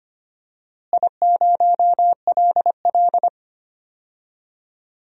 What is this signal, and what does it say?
Morse code, heard as a keyed tone, about 25 wpm, 700 Hz. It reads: I0LL
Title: DXCC-100 Call Signs - 4 Characters - Repeated 2x - 25wpm